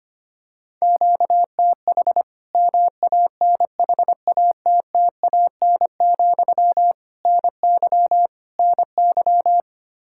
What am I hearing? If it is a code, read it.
Morse code, heard as a keyed tone, about 25 wpm, 700 Hz. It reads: QTH MANHATTAN, NY NY
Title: Practice QSO - Exchange 1 - Beginning - 25wpm